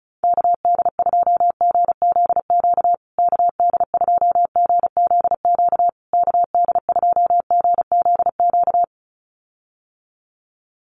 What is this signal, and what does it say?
Morse code, heard as a keyed tone, about 34 wpm, 700 Hz. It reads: KD2GZQ KD2GZQ KD2GZQ
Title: USA Call Signs Repeated 3x 35wpm